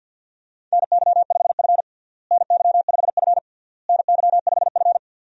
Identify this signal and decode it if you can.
Morse code, heard as a keyed tone, about 52 wpm, 700 Hz. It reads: NY5F NY5F NY5F
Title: USA Call Signs - 4 Characters - Repeated 3x - 50wpm